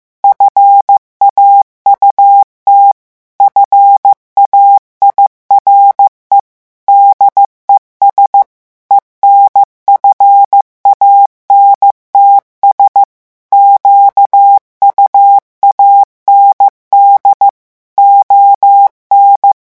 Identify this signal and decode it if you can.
Morse code, heard as a keyed tone, about 15 wpm, 800 Hz. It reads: FAUT FAIRE DES ENFANTS QUAND ON
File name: morse mp3 / revision9 citations humoristiques vitesse 14 mots minute